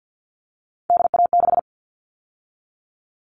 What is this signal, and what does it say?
Morse code, heard as a keyed tone, about 50 wpm, 700 Hz. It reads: DA6
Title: All Cut Numbers - 3-Digit 50wpm